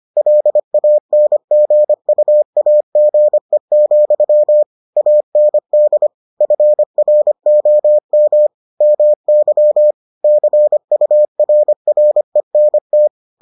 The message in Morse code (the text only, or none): LANGUAGE, AND FROM MY CURRENT